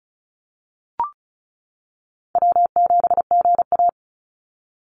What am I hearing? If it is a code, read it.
Morse code, heard as a keyed tone, about 36 wpm, 700 Hz. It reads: W7GA
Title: NA SOTA Chaser Callsigns 35wpm